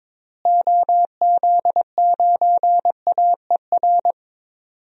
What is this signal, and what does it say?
Morse code, heard as a keyed tone, about 22 wpm, 700 Hz. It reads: OZ9AER
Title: DXCC-100 Call Signs - 6 Characters - Repeated 2x - 22wpm